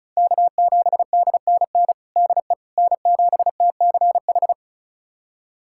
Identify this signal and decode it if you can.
Morse code, heard as a keyed tone, about 36 wpm, 700 Hz. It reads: K7DNN DE N7TCH
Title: USA Callsign DE Callsign - 5 Characters - 35wpm